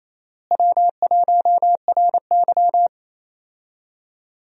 Morse code, heard as a keyed tone, about 28 wpm, 700 Hz. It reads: W1RY
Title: USA Call Signs - 4 Characters - Repeated 3x - 28wpm